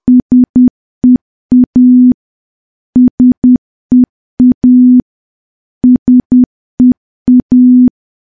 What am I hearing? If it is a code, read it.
Morse code, heard as a keyed tone, about 10 wpm, 250 Hz. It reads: SEA SEA SEA
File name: morse / morse-training-10WPM